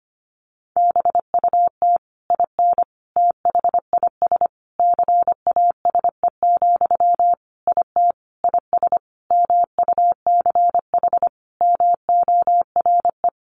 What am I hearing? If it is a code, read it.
Morse code, heard as a keyed tone, about 25 wpm, 700 Hz. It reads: BUT IN THIS CASE, IT IS MUCH MORE
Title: Sentences from Top 1200 Words 25wpm